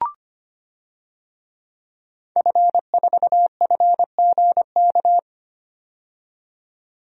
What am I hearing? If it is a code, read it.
Morse code, heard as a keyed tone, about 25 wpm, 690 Hz. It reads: F4FGK
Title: DXCC-100 Call Signs - 5 Characters - 25wpm